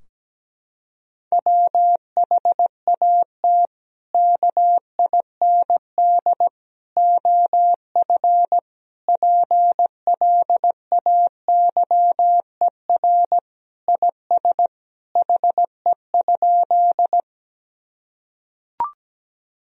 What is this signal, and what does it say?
Morse code, heard as a keyed tone, about 17 wpm, 700 Hz. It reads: WHAT KIND OF PLAYER IS HE?
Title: Sentences from Top 1000 Words - Review Part 4 - 17wpm